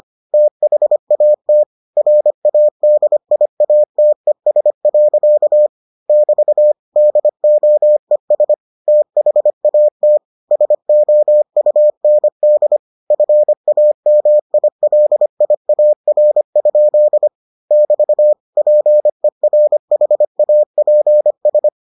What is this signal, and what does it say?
Morse code, heard as a keyed tone, about 25 wpm, 600 Hz. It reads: THAT RADIATES. <BT> DOES THAT SOUND FAMILIAR? <BT> PERHAPS